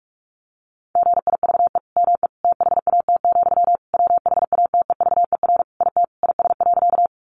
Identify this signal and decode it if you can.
Morse code, heard as a keyed tone, about 44 wpm, 700 Hz. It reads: GIVE ME THAT, WHATEVER IT IS.
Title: Sentences from Top 1100 Words 45wpm